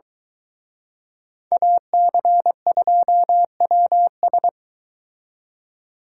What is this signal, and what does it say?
Morse code, heard as a keyed tone, about 22 wpm, 700 Hz. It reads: AC2WS